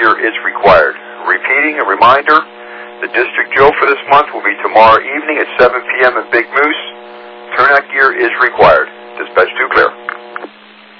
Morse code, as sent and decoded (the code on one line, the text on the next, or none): none